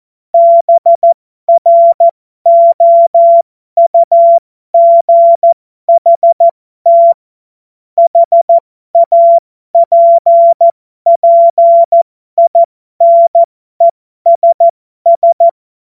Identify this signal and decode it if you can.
Morse code, heard as a keyed tone, about 14 wpm, spaced as for 12 wpm, 680 Hz. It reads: BROUGHT HAPPINESS
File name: Morsekurs / Witze